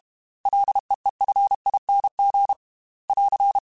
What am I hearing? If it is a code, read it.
Morse code, heard as a keyed tone, about 32 wpm, 800 Hz. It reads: LEEFING <AR>
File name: Ls / Single L m3s / Single L @30